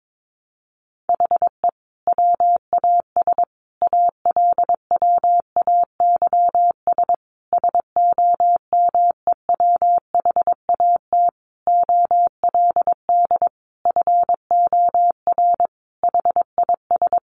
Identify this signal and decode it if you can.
Morse code, heard as a keyed tone, about 22 wpm, 700 Hz. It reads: HE WAS ALWAYS SOMEWHAT OLD FOR HIS
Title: Sentences from Top 1300 Words 22wpm